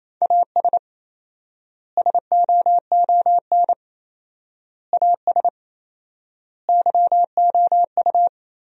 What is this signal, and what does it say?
Morse code, heard as a keyed tone, about 27 wpm, 700 Hz. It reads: AS SOON AS YOU
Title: 4x Word-Spacing - Sentences from Top 400 Words - 28wpm